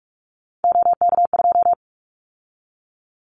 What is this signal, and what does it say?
Morse code, heard as a keyed tone, about 46 wpm, 700 Hz. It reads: OK2
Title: Callsign Prefixes for Top 25 DXCC - 45wpm